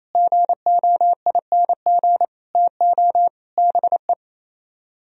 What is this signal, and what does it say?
Morse code, heard as a keyed tone, about 37 wpm, 700 Hz. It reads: GOING TO BE